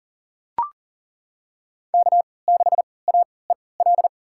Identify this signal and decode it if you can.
Morse code, heard as a keyed tone, about 39 wpm, spaced as for 22 wpm, 700 Hz. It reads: K6AEL
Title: USA Call Signs - 3x Character Spacing - 40wpm